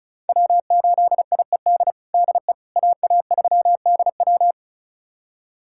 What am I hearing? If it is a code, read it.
Morse code, heard as a keyed tone, about 36 wpm, 700 Hz. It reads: W8IED DE AA3DW
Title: USA Callsign DE Callsign - 5 Characters - 35wpm